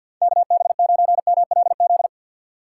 Morse code, heard as a keyed tone, about 51 wpm, 700 Hz. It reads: KB9MLZ
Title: USA Call Signs - 6 Characters - 3x Character Spacing - 50wpm